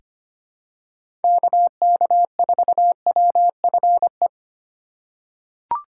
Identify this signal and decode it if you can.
Morse code, heard as a keyed tone, about 25 wpm, 700 Hz. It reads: KK4WFE
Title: USA Call Signs - 8x Character Spacing - 25wpm